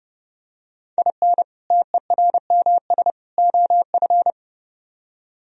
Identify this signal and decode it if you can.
Morse code, heard as a keyed tone, about 31 wpm, 700 Hz. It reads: IN TERMS OF